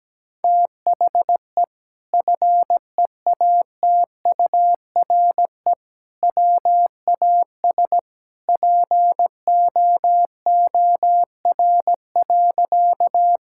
Morse code, heard as a keyed tone, about 17 wpm, 700 Hz. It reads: THE FEATURE WAS POOR.
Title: Sentences from Top 1000 Words - Review Part 5 - 17wpm